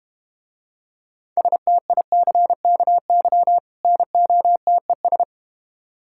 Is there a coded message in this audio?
Yes: Morse code from 1.4 to 5.2 s, heard as a keyed tone, about 32 wpm, 700 Hz. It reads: STICKY NOTES